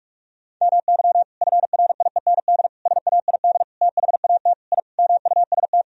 Morse code, heard as a keyed tone, about 46 wpm, 700 Hz. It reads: MY FRIEND SAID THAT I MUST
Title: Sentences from Top 1000 Words - Review Part 4 - 45wpm